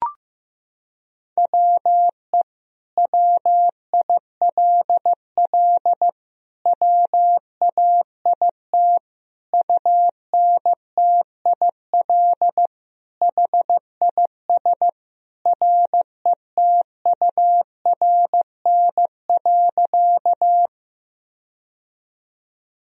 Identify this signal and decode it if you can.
Morse code, heard as a keyed tone, about 15 wpm, 700 Hz. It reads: WE WILL WAIT UNTIL HIS RETURN.
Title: Sentences from Top 1000 Words - Review Part 1 - 15wpm